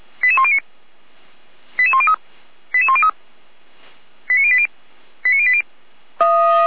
Five-tone selective call (ZVEI), heard as quick runs of five tones, tones of about 70 ms. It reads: 88109 88103 88103 89080 89080
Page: /455/SDR/ZVEI/88104/